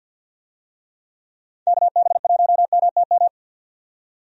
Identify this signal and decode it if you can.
Morse code, heard as a keyed tone, about 47 wpm, 700 Hz. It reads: KB1MTM
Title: USA Call Signs - 6 Characters - 50wpm